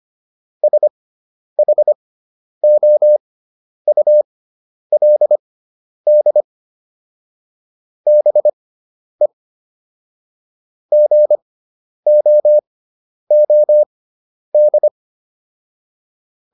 Morse code, heard as a keyed tone, about 25 wpm, spaced as for 10 wpm, 600 Hz. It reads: SHOULD BE GOOD